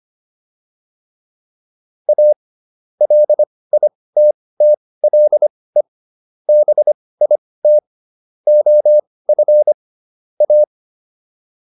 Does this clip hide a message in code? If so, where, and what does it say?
Yes: Morse code from 2.1 to 10.6 s, heard as a keyed tone, about 25 wpm, spaced as for 18 wpm, 600 Hz. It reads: A LITTLE BIT OF A